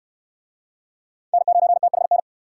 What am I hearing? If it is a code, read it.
Morse code, heard as a keyed tone, about 59 wpm, 700 Hz. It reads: N9THN